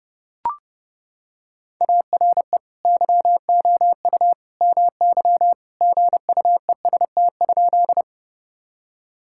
Morse code, heard as a keyed tone, about 30 wpm, 700 Hz. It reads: ARE YOU MY GUEST?
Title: Sentences from Top 1600 Words 30wpm